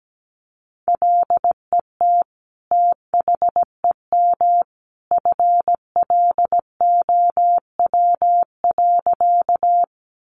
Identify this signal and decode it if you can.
Morse code, heard as a keyed tone, about 17 wpm, 700 Hz. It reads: LET THEM FLOW.
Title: Sentences from Top 1200 Words 17wpm